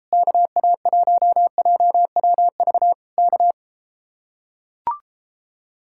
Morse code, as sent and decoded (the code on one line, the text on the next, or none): -.- .- .---- .--- .-- ...- / -.-
KA1JWV K